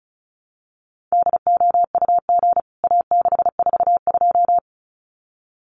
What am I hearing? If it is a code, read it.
Morse code, heard as a keyed tone, about 34 wpm, 700 Hz. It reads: DOUG A642
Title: CWT Exchange 35wpm